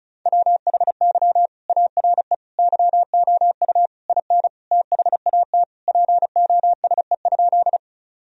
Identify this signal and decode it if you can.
Morse code, heard as a keyed tone, about 34 wpm, 700 Hz. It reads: WHY ARE YOU IN THAT POSE?
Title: Sentences from Top 1000 Words - Review Part 2 - 35wpm